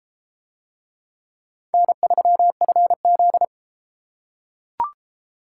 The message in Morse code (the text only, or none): N3FZ